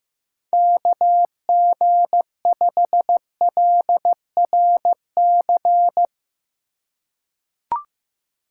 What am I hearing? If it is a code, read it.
Morse code, heard as a keyed tone, about 15 wpm, 700 Hz. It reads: KG5LRC
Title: USA Call Signs - 6 Characters - 4x Character Spacing - 15wpm